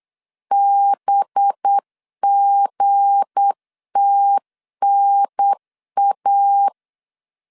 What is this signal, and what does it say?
Morse code, heard as a keyed tone, about 8 wpm, 800 Hz. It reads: BGTNA
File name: LEZIONI / 8 YJ45 / 8 Y 8wpm 3245